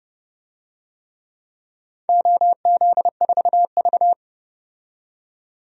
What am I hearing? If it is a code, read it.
Morse code, heard as a keyed tone, about 31 wpm, 700 Hz. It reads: OZ4V
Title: DXCC-100 Call Signs - 4 Characters - 30wpm